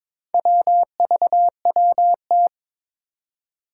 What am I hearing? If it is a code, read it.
Morse code, heard as a keyed tone, about 22 wpm, 700 Hz. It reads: WVWT
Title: Four Challenging Letter-Number Combinations 22wpm